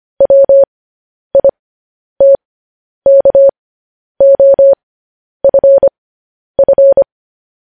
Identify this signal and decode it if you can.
Morse code, heard as a keyed tone, about 25 wpm, spaced as for 10 wpm, 550 Hz. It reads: WITKOFF
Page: news headlines in morse code at 10 wpm